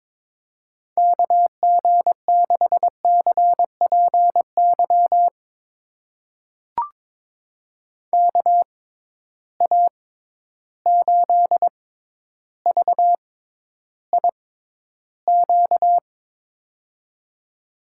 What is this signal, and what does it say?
Morse code, heard as a keyed tone, about 22 wpm, spaced as for 11 wpm, 700 Hz. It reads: KG6CPY KA8VIQ
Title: USA Call Signs - 6 Characters - 6x Character Spacing - 22wpm